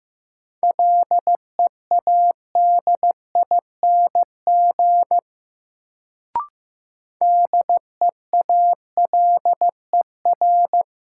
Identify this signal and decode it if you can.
Morse code, heard as a keyed tone, about 15 wpm, 700 Hz. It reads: LEADING DEALER
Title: Sentences from Top 1100 Words - New Word Drill - 15wpm